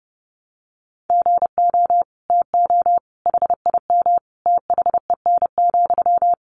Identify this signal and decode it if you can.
Morse code, heard as a keyed tone, about 30 wpm, 700 Hz. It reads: GO TO HIM THEN,
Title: Sentences from Top 1100 Words 30wpm